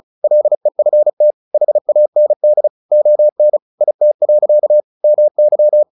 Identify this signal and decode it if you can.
Morse code, heard as a keyed tone, about 34 wpm, 600 Hz. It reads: LEFT HAND ON IT. MY
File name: the old man and the sea 2 35wpm